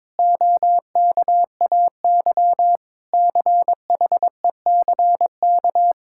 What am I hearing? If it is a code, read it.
Morse code, heard as a keyed tone, about 22 wpm, 700 Hz. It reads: OKAY CHECK
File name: practice / 22wpm / Sentences with 4-Word Phrases - 22wpm